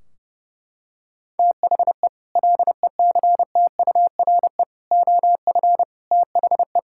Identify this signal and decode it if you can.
Morse code, heard as a keyed tone, about 32 wpm, 700 Hz. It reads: THE LECTURE OF THE
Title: Sentences from Top 1600 Words 30wpm